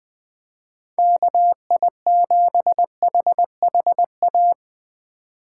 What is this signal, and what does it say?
Morse code, heard as a keyed tone, about 20 wpm, 700 Hz. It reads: KI7HHA
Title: USA Call Signs - 2x Character Spacing - 20wpm